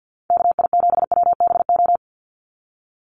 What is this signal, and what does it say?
Morse code, heard as a keyed tone, about 55 wpm, 700 Hz. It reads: KI7WBO
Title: USA Call Signs 50wpm